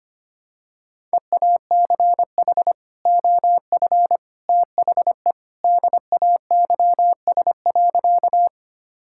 Morse code, heard as a keyed tone, about 25 wpm, 700 Hz. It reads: EACH OF THE DAYS.
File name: practice / 25wpm / Sentences with 3-Word Phrases - 25wpm